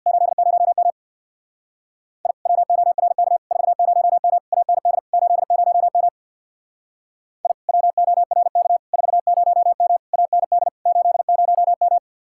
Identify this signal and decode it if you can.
Morse code, heard as a keyed tone, about 60 wpm, 700 Hz. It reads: 80M I WORK 40M AND 80M I WORK 40M AND 80M
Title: Speed-Racing Common QSO Phrases - 50wpm